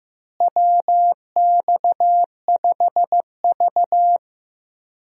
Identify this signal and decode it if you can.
Morse code, heard as a keyed tone, about 15 wpm, 700 Hz. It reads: WX5V
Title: USA Call Signs - 4 Characters - Repeated 2x - 15wpm